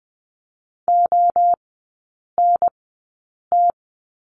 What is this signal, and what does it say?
Morse code, heard as a keyed tone, about 20 wpm, spaced as for 8 wpm, 700 Hz. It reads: ONT
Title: 053 Lesson 10e - Letter r - Three Characters 20wpm